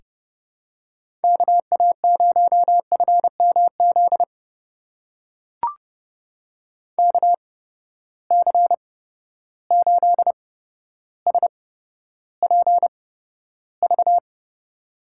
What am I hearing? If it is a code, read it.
Morse code, heard as a keyed tone, about 30 wpm, spaced as for 13 wpm, 700 Hz. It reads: KA0FMZ KC8SPV
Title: USA Call Signs - 6 Characters - 8x Character Spacing - 30wpm